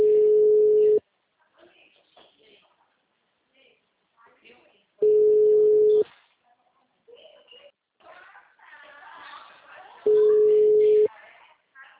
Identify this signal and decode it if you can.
Ringback tone (Europe, 425 Hz), with speech.